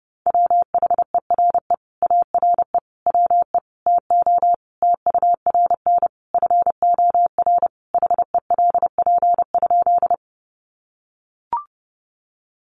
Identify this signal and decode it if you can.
Morse code, heard as a keyed tone, about 30 wpm, 700 Hz. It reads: WHERE ARE WE TO TURN FOR HELP?
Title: Sentences from Top 200 Words 30wpm